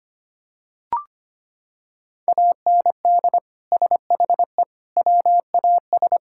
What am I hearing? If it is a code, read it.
Morse code, heard as a keyed tone, about 25 wpm, 700 Hz. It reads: AND SHE WAS
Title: Sentences from Top 1000 Words - Review Part 5 - 25wpm